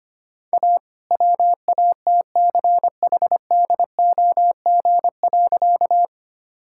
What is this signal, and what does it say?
Morse code, heard as a keyed tone, about 25 wpm, 700 Hz. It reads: A WATCHDOG.
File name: practice / 25wpm / All-Age Appropriate Jokes - 25wpm